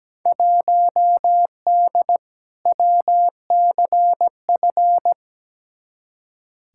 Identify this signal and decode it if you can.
Morse code, heard as a keyed tone, about 17 wpm, 700 Hz. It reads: 1D WCF
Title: Field Day Exchanges - 17wpm